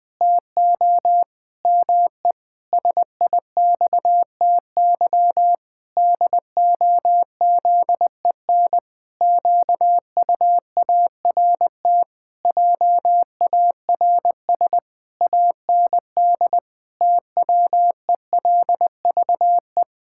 Morse code, heard as a keyed tone, about 20 wpm, 700 Hz. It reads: TO ME SIXTY DOZEN QUART JARS AND TWELVE
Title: Pangram Challenge v2 - 20wpm